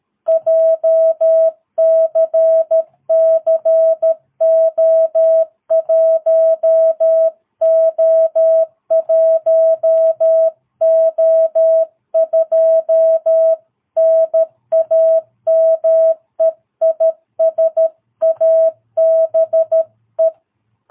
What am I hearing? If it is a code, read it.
Morse code, heard as a keyed tone, about 13 wpm, 660 Hz. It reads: JCCO1O1O2 NAME IS ABE
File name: jh8cbh / 2022cw